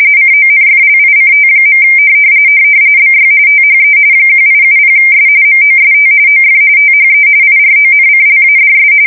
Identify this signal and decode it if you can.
RTTY, about 45.45 baud. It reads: CONFLICT. BY THE 1870S MOST OF THE IRONCLADS
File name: fsk170 / usnavy3